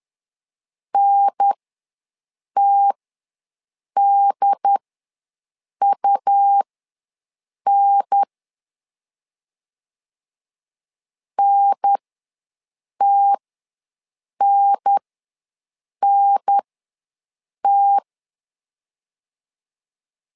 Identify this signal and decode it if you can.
Morse code, heard as a keyed tone, about 10 wpm, spaced as for 6 wpm, 800 Hz. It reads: NTDUN NTNNT
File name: 2 UHDN ws 10wpm